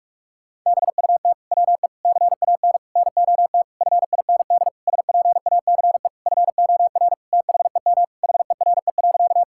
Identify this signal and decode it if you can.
Morse code, heard as a keyed tone, about 41 wpm, 700 Hz. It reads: BUT WE CAN NOT FIND SPACE FOR THEM HERE.
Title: Sentences from Top 1000 Words - Review Part 5 - 45wpm